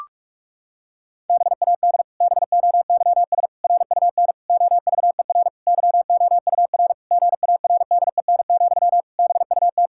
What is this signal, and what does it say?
Morse code, heard as a keyed tone, about 46 wpm, 700 Hz. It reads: BAD BOYS RUN OVER YOUR GARDEN, BUT
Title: Pangram Challenge v2 - 45wpm